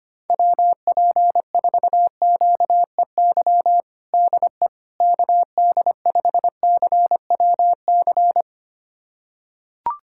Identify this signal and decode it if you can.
Morse code, heard as a keyed tone, about 25 wpm, 700 Hz. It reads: WP4QEY DE KD5CWC
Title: USA Callsign DE Callsign 25wpm